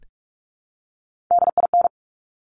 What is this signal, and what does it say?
Morse code, heard as a keyed tone, about 46 wpm, 700 Hz. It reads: DIN
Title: Top Three Letter Combinations in Words 45wpm